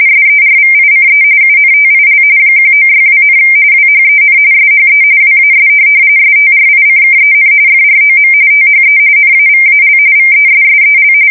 RTTY, about 45.45 baud. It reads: SERVICE UNTIL 1962, WHEN SHE WAS STRICKEN FROM THE NAVAL VESSEL
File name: fsk170 / USS-Massachusetts